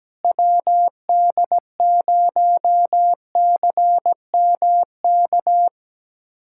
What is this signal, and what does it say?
Morse code, heard as a keyed tone, about 17 wpm, 700 Hz. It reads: WD0CMK
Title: USA Callsign DE Callsign - 6 Characters - 17wpm